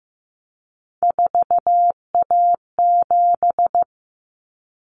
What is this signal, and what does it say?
Morse code, heard as a keyed tone, about 15 wpm, 700 Hz. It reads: 4A7
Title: Common Cut Numbers - 3-Digit 15wpm